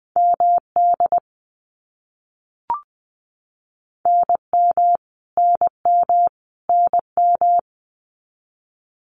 Morse code, heard as a keyed tone, about 20 wpm, 700 Hz. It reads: MD NM NM NM
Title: US States Repeated 3x 20wpm